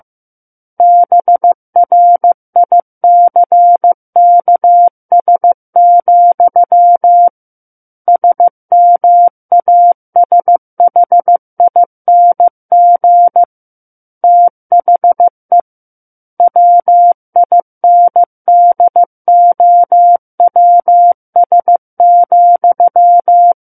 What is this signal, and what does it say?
Morse code, heard as a keyed tone, about 15 wpm, 700 Hz. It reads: BRICKS, SMASHING THE WINDOWS,
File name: Chapter 6 - 15 WPM